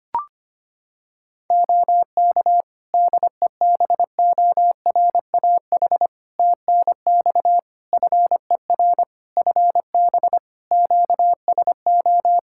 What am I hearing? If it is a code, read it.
Morse code, heard as a keyed tone, about 25 wpm, 700 Hz. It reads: OK DEBORAH TNX FER FB QSO
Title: Practice QSO - Exchange 4 - Ending - 25wpm